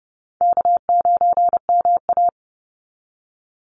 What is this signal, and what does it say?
Morse code, heard as a keyed tone, about 30 wpm, 700 Hz. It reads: K9MA
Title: Call Signs Repeated 2x - 30wpm